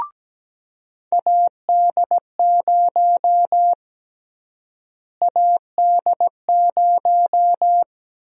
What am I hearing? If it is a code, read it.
Morse code, heard as a keyed tone, about 17 wpm, 700 Hz. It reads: AD0 AD0
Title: USA Call Sign Prefixes - Repeated 2x - 17wpm